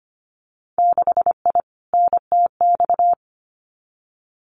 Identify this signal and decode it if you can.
Morse code, heard as a keyed tone, about 25 wpm, 700 Hz. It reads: 6I NTX
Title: Winter Field Day Exchanges 25wpm